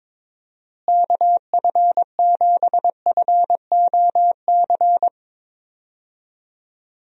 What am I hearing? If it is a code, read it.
Morse code, heard as a keyed tone, about 22 wpm, 700 Hz. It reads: KF7FOC
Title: USA Call Signs - 6 Characters - Repeated 2x - 22wpm